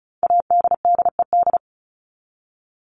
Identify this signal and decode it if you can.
Morse code, heard as a keyed tone, about 34 wpm, 700 Hz. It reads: ADDED
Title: ED Suffix - Top 100 Words with Base Words 35wpm